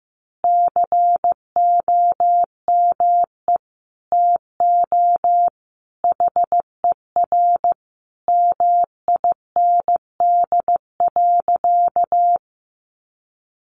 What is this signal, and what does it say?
Morse code, heard as a keyed tone, about 14 wpm, 700 Hz. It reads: COME TO HER MIND.
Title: Sentences from Top 1300 Words 15wpm